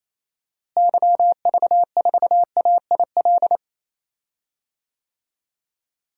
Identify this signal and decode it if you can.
Morse code, heard as a keyed tone, about 28 wpm, 700 Hz. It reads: YV4AIL
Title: DXCC-100 Call Signs - 6 Characters - Repeated 2x - 28wpm